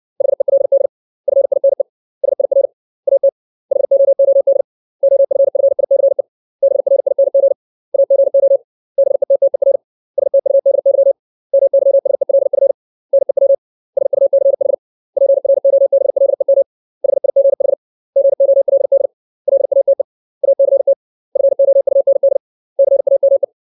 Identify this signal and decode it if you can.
Morse code, heard as a keyed tone, about 60 wpm, 550 Hz. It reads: HELD FINE SIR AT HOOD OFFICE BRING AGO BETTER STUDY MYSELF NEW SUCH PROBLEM HIGH GOLD LATE ACT FOUND CAME
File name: 632 words / 632 Words 60WPM